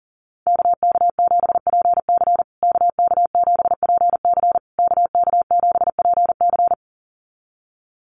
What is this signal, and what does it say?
Morse code, heard as a keyed tone, about 40 wpm, 700 Hz. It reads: KK7PC KK7PC KK7PC
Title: USA Call Signs Repeated 3x 40wpm